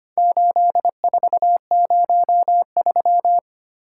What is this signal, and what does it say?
Morse code, heard as a keyed tone, about 25 wpm, 700 Hz. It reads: 8403
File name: practice / 25wpm / Four Letter-Number Combinations - 25wpm